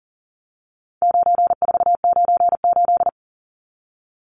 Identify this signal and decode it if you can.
Morse code, heard as a keyed tone, about 37 wpm, 700 Hz. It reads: 9498
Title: Numbers - 4-Digit 40wpm